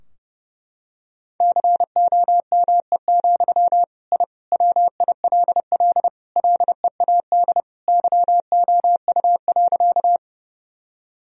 Morse code, heard as a keyed tone, about 30 wpm, 700 Hz. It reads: COME, I WILL LEAD YOU.
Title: Sentences from Top 1000 Words - Review Part 2 - 30wpm